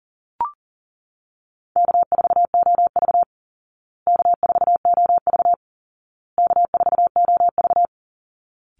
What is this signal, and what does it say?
Morse code, heard as a keyed tone, about 40 wpm, 700 Hz. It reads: K4OV K4OV K4OV
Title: Call Signs Repeated 2x - 40wpm